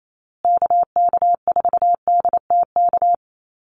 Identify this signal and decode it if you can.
Morse code, heard as a keyed tone, about 28 wpm, 700 Hz. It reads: KK4DTK
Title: USA Call Signs Repeated 2x 28wpm